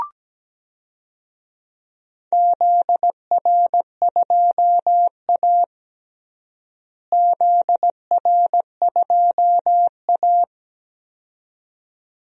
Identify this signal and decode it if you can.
Morse code, heard as a keyed tone, about 17 wpm, 700 Hz. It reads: ZR2A ZR2A
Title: DXCC-100 Call Signs - 4 Characters - Repeated 2x - 17wpm